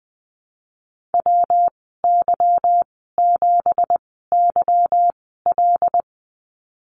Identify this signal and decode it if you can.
Morse code, heard as a keyed tone, about 20 wpm, spaced as for 15 wpm, 700 Hz. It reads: WY7YL
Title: 173 Lesson 28g - Number 9 - Call Signs 20wpm